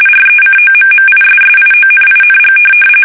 RTTY, about 45.45 baud. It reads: RECOMMISSIONING
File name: fsk850 / battleshipnj